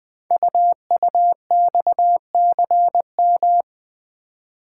Morse code, heard as a keyed tone, about 20 wpm, 700 Hz. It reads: UUXCM
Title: Code Groups - 5 Letters - 20wpm